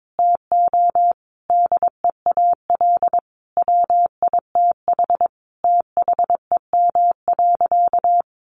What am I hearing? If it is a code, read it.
Morse code, heard as a keyed tone, about 22 wpm, 700 Hz. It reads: TO DEAL WITH THEM.
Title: Sentences from Top 800 Words 22wpm